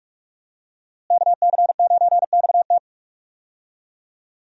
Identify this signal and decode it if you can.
Morse code, heard as a keyed tone, about 46 wpm, 700 Hz. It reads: KC9XT
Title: USA Call Signs - 5 Characters - Repeated 2x - 45wpm